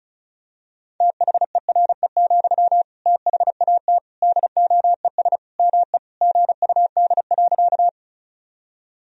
Morse code, heard as a keyed tone, about 36 wpm, 700 Hz. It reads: THERE, THAT DOES ME GUD.
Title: Sentences from Top 200 Words and Abbreviations 35wpm